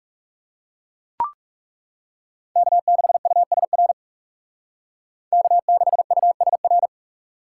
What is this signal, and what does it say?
Morse code, heard as a keyed tone, about 36 wpm, 700 Hz. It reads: K6USR K6USR
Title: Speed Racing - USA 1x3 Call Signs 30wpm